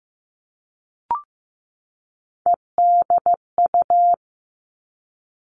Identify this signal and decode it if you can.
Morse code, heard as a keyed tone, about 15 wpm, 700 Hz. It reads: EDU
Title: Three Challenging Letter-Number Combinations 15wpm